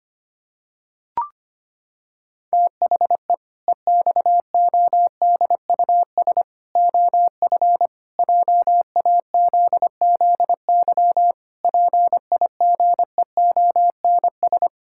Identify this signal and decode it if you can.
Morse code, heard as a keyed tone, about 25 wpm, 700 Hz. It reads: THE EXODUS OF JAZZY PIGEONS
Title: Pangram Challenge v2 - 25wpm